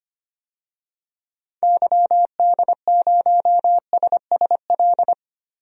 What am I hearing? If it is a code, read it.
Morse code, heard as a keyed tone, about 25 wpm, 700 Hz. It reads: YD0SSL
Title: DXCC-100 Call Signs - 6 Characters - Repeated 2x - 25wpm